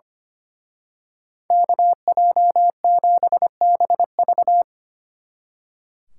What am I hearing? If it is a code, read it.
Morse code, heard as a keyed tone, about 28 wpm, 700 Hz. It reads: KJ7BV
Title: Speed Racing - USA 2x2 Call Signs 25wpm